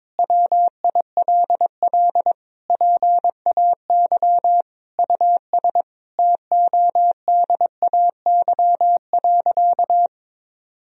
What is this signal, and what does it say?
Morse code, heard as a keyed tone, about 22 wpm, 700 Hz. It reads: WILL PAY US TODAY.
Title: Sentences from Top 1000 Words - Review Part 4 - 22wpm